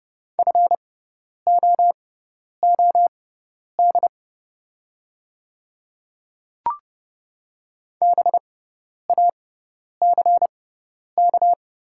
Morse code, heard as a keyed tone, about 30 wpm, spaced as for 10 wpm, 690 Hz. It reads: FOOD BACK
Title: 6x Character Spacing - Top 300 Words - 30wpm